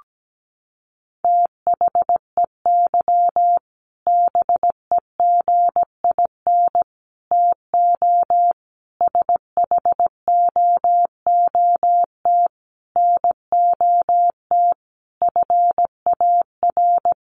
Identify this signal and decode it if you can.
Morse code, heard as a keyed tone, about 17 wpm, 700 Hz. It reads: THEY BEGIN TO SHOOT NOT FAR
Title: Sentences from Top 1200 Words 17wpm